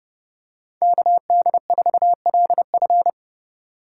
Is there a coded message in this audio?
Yes: Morse code from 0.8 to 3.1 s, heard as a keyed tone, about 27 wpm, 700 Hz. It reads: KD4LF